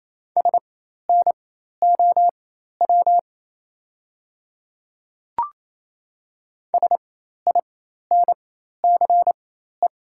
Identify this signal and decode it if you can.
Morse code, heard as a keyed tone, about 28 wpm, spaced as for 13 wpm, 690 Hz. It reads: SNOW SINCE
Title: 4x Character Spacing - Top 500 Words - 28wpm